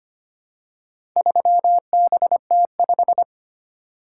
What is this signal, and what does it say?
Morse code, heard as a keyed tone, about 25 wpm, 700 Hz. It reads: 3BT5
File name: practice / 25wpm / Four Letter-Number Combinations - 25wpm